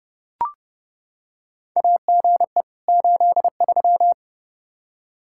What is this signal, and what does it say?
Morse code, heard as a keyed tone, about 30 wpm, 700 Hz. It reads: AGE 83
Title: K1USN SST QSO Practice 30wpm